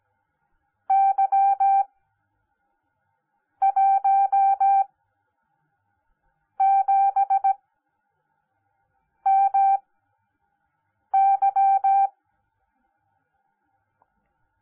Morse code, heard as a keyed tone, about 17 wpm, spaced as for 5 wpm, 780 Hz. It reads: Y17MY
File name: Onlinelehrgang / morsen / DJ4UF Morsen10